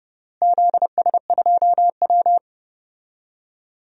Morse code, heard as a keyed tone, about 32 wpm, 700 Hz. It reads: ZS2W
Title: DXCC-100 Call Signs - 4 Characters - Repeated 3x - 30wpm